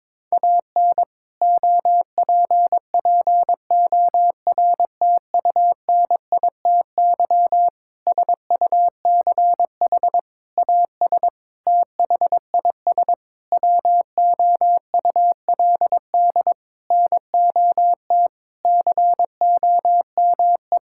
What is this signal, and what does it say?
Morse code, heard as a keyed tone, about 22 wpm, 700 Hz. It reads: AN OPPORTUNITY SUCH AS THIS WOULD NOT COME
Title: Sentences from Top 1600 Words 22wpm